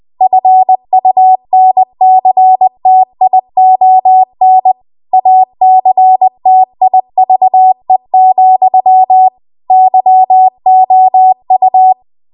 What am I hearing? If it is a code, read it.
Morse code, heard as a keyed tone, about 20 wpm, 750 Hz. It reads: FUNCTION ACTIVE, YOU
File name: Archive / 20 WPM / 20WPM